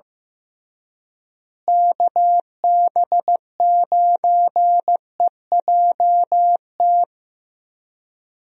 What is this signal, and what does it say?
Morse code, heard as a keyed tone, about 15 wpm, 700 Hz. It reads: KB9EJT
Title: USA Call Signs - 6 Characters - Repeated 3x - 15wpm